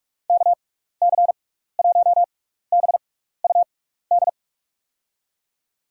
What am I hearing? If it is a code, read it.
Morse code, heard as a keyed tone, about 44 wpm, spaced as for 15 wpm, 700 Hz. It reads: KC1BUD
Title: USA Call Signs - 6x Character Spacing - 45wpm